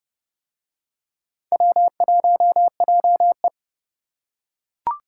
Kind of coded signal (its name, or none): Morse code